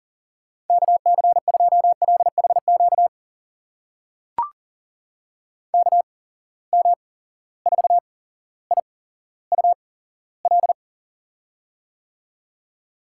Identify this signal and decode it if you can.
Morse code, heard as a keyed tone, about 40 wpm, spaced as for 17 wpm, 700 Hz. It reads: KC2LHQ KM4IUL
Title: USA Call Signs - 6 Characters - 8x Character Spacing - 40wpm